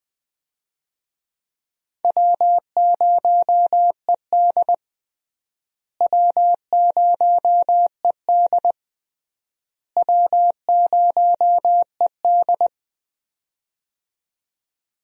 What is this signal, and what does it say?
Morse code, heard as a keyed tone, about 20 wpm, 700 Hz. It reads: W0ED W0ED W0ED
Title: USA Call Signs - 4 Characters - Repeated 3x - 20wpm